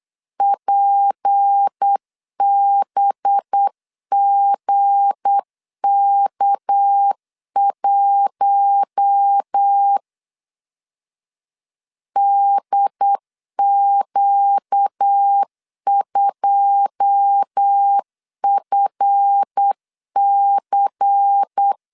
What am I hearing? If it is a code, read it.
Morse code, heard as a keyed tone, about 8 wpm, 800 Hz. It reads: PBGK1 DQ2FC
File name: LEZIONI / 9 2 8wpm 3437